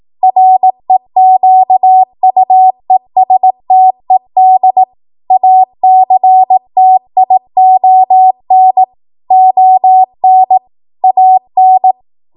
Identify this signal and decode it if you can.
Morse code, heard as a keyed tone, about 18 wpm, 750 Hz. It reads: REQUESTED ACTION ON AN
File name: Morse / 18WPM